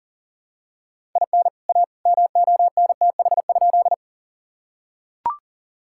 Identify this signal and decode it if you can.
Morse code, heard as a keyed tone, about 40 wpm, 700 Hz. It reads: IN A MONTH?
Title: Sentences from Top 1000 Words - Review Part 5 - 40wpm